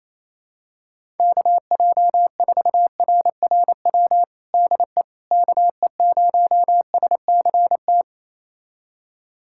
Morse code, heard as a keyed tone, about 28 wpm, 700 Hz. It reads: KJ4RRW DE KE0SCT
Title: USA Callsign DE Callsign - 6 Characters - 28wpm